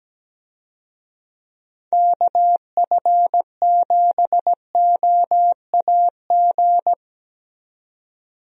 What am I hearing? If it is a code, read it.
Morse code, heard as a keyed tone, about 17 wpm, 700 Hz. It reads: KF7OAG
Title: USA Call Signs - 6 Characters - 17wpm